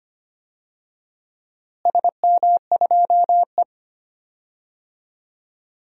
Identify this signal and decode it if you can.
Morse code, heard as a keyed tone, about 25 wpm, 700 Hz. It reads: SM2E
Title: DXCC-100 Call Signs - 4 Characters - Repeated 2x - 25wpm